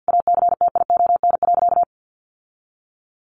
Morse code, heard as a keyed tone, about 50 wpm, 700 Hz. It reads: ACTION.
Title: Sentences from Top 700 Words 50wpm